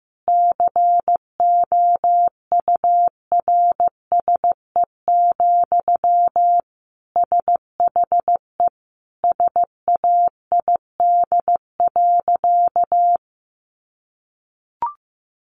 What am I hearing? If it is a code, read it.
Morse code, heard as a keyed tone, about 15 wpm, 700 Hz. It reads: COURSE, SHE SAID.
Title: Sentences from Top 1400 Words 15wpm